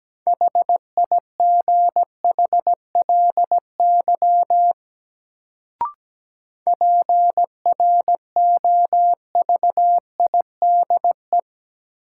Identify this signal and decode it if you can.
Morse code, heard as a keyed tone, about 17 wpm, 700 Hz. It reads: HIGHLY PROVIDE
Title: Sentences from Top 1100 Words - New Word Drill Alt - 17wpm